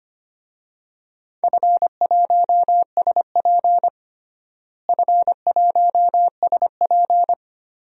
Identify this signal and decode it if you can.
Morse code, heard as a keyed tone, about 25 wpm, 700 Hz. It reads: F1SP F1SP
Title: DXCC-100 Call Signs - 4 Characters - Repeated 2x - 25wpm